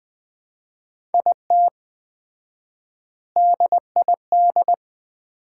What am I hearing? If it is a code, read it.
Morse code, heard as a keyed tone, about 20 wpm, 700 Hz. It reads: IT DID
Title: 4x Word-Spacing - Sets of 3 Words - 20wpm